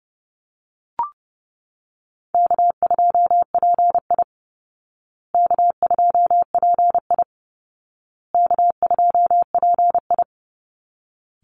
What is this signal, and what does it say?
Morse code, heard as a keyed tone, about 30 wpm, 700 Hz. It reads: K2PI K2PI K2PI
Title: Easy Call Signs Repeated 3x - 30wpm